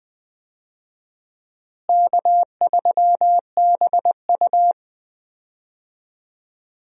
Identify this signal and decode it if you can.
Morse code, heard as a keyed tone, about 21 wpm, 700 Hz. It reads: K3BU
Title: USA Call Signs - 4 Characters - Repeated 3x - 20wpm